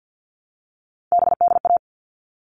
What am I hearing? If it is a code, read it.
Morse code, heard as a keyed tone, about 50 wpm, 700 Hz. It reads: BDA
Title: All Cut Numbers - 3-Digit 50wpm